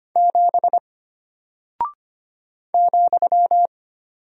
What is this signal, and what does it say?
Morse code, heard as a keyed tone, about 25 wpm, 700 Hz. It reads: 7 ,
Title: Single Character (Letter:Number:Punctuation:Prosign) Alt 25wpm